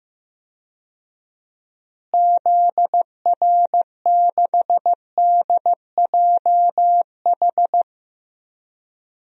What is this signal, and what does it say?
Morse code, heard as a keyed tone, about 15 wpm, 700 Hz. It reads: ZR6DJH
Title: DXCC-100 Call Signs - 6 Characters - Repeated 2x - 15wpm